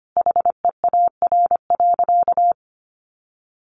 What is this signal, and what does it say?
Morse code, heard as a keyed tone, about 25 wpm, 700 Hz. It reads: HEAR.
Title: Sentences from Top 600 Words 25wpm